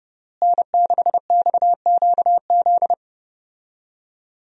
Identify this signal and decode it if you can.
Morse code, heard as a keyed tone, about 30 wpm, 700 Hz. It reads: N6XQZ
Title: USA Call Signs - 5 Characters - Repeated 2x - 30wpm